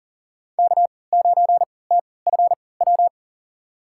Morse code, heard as a keyed tone, about 40 wpm, spaced as for 23 wpm, 700 Hz. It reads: K9TFW
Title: USA Call Signs - 5 Characters - 3x Character Spacing - 40wpm